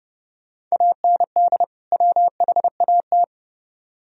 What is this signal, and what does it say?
Morse code, heard as a keyed tone, about 36 wpm, 700 Hz. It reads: AND WHAT
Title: Speed-Racing - Sets of 2 Words 30wpm